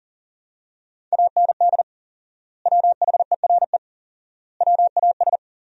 Morse code, heard as a keyed tone, about 40 wpm, 700 Hz. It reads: AND WHERE WAS
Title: 4x Word-Spacing - Sentences from Top 200 Words - 40wpm